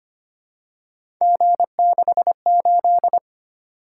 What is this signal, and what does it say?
Morse code, heard as a keyed tone, about 25 wpm, 700 Hz. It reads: G68
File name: Three Letter-Number Combinations - 25wpm